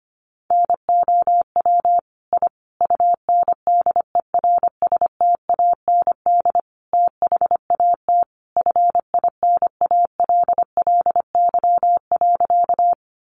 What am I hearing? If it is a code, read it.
Morse code, heard as a keyed tone, about 25 wpm, 700 Hz. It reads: NOW I UNDERSTAND THAT FINALLY.
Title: Sentences from Top 1000 Words 25wpm